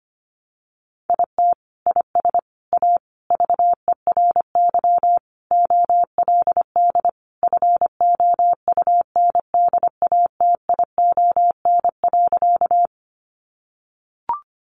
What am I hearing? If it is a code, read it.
Morse code, heard as a keyed tone, about 25 wpm, 700 Hz. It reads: IT IS A VERY OLD FOUNDATION.
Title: Sentences from Top 1200 Words 25wpm